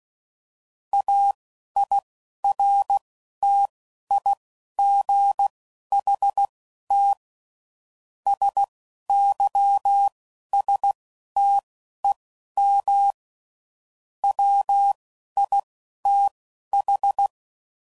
Morse code, heard as a keyed tone, about 16 wpm, spaced as for 11 wpm, 800 Hz. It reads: AIRTIGHT SYSTEM WITH